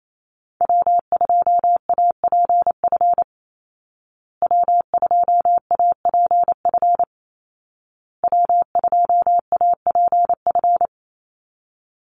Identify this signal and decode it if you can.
Morse code, heard as a keyed tone, about 28 wpm, 700 Hz. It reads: W2APF W2APF W2APF
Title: Call Signs Repeated 2x - 28wpm